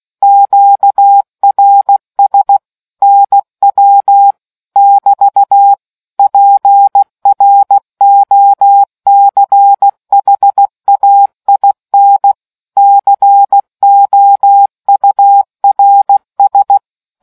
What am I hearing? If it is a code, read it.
Morse code, heard as a keyed tone, about 16 wpm, 800 Hz. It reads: QRS NW <BT> PROCHAIN COURS